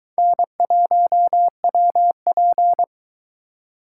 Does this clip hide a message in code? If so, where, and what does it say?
Yes: Morse code from 0.2 to 2.8 s, heard as a keyed tone, about 22 wpm, 700 Hz. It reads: N1WP